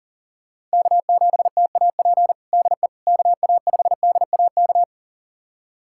Morse code, heard as a keyed tone, about 40 wpm, 700 Hz. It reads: K7TAP DE KA5DAK
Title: USA Callsign DE Callsign 40wpm